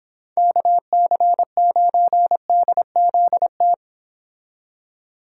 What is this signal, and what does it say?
Morse code, heard as a keyed tone, about 26 wpm, 700 Hz. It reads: KC9DZT